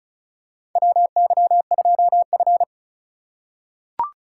Morse code, heard as a keyed tone, about 36 wpm, 700 Hz. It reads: WY2F
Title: USA Call Signs - 4x Character Spacing - 35wpm